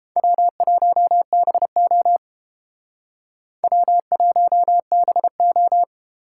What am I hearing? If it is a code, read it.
Morse code, heard as a keyed tone, about 28 wpm, 700 Hz. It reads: W1BO W1BO